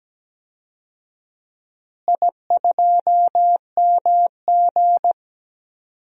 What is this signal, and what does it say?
Morse code, heard as a keyed tone, about 17 wpm, 700 Hz. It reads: I2MG
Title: DXCC-100 Call Signs - 4 Characters - Repeated 2x - 17wpm